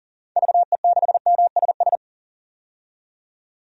Morse code, heard as a keyed tone, about 40 wpm, 700 Hz. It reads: VE6MSS
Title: DXCC-100 Call Signs - 6 Characters - 40wpm